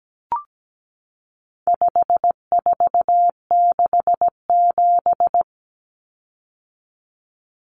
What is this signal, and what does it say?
Morse code, heard as a keyed tone, about 17 wpm, 700 Hz. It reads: 5467
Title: Common Cut Numbers - 4-Digit 17wpm